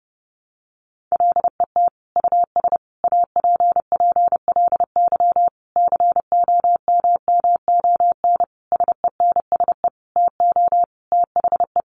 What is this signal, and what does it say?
Morse code, heard as a keyed tone, about 30 wpm, 700 Hz. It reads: LET US APPLY COMMON SENSE TO THE
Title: Sentences from Top 1200 Words 30wpm